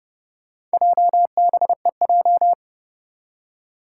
Morse code, heard as a keyed tone, about 29 wpm, 700 Hz. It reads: JBEJ
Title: Four Challenging Letter-Number Combinations 30wpm